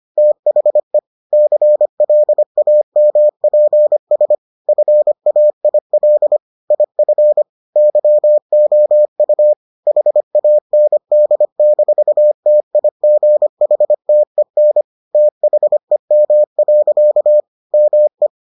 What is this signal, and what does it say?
Morse code, heard as a keyed tone, about 25 wpm, 600 Hz. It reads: THE CLAMPS FAIL IF YOU HAND-TIGHTEN THEM. ME